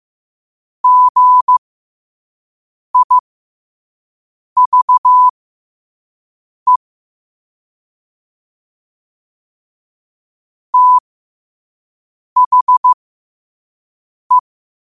Morse code, heard as a keyed tone, about 15 wpm, spaced as for 5 wpm, 1000 Hz. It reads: GIVE THE
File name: qotdfiles / 5wpm